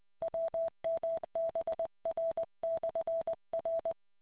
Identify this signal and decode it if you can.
Morse code, heard as a keyed tone, about 20 wpm, 660 Hz. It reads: WMBR/R